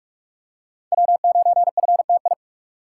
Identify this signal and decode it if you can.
Morse code, heard as a keyed tone, about 46 wpm, 700 Hz. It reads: W9FTI
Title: USA Call Signs - 5 Characters - 2x Character Spacing - 45wpm